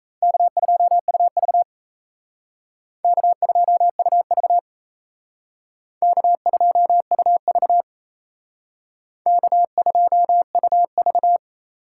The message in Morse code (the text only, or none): K2UV K2UV K2UV K2UV